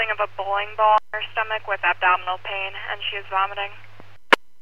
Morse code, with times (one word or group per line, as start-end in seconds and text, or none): none